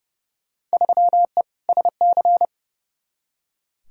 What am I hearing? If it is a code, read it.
Morse code, heard as a keyed tone, about 30 wpm, 700 Hz. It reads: 3E SC
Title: Field Day Exchanges Encore - 30wpm